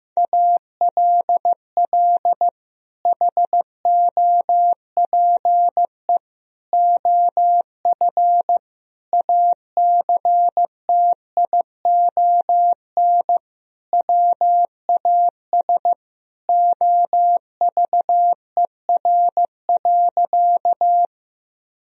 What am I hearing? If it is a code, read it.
Morse code, heard as a keyed tone, about 16 wpm, 700 Hz. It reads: ALL HOPE OF ACTION WAS OVER.
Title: Sentences from Top 1000 Words - Review Part 4 - 15wpm